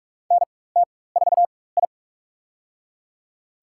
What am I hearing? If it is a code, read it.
Morse code, heard as a keyed tone, about 46 wpm, spaced as for 21 wpm, 700 Hz. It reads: NT4I
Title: USA Call Signs - 4 Characters - 4x Character Spacing - 45wpm